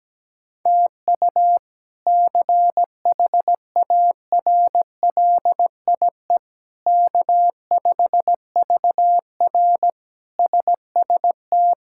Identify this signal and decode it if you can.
Morse code, heard as a keyed tone, about 17 wpm, 700 Hz. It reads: TU CHARLIE K5VR SST
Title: K1USN SST QSO Practice - 17wpm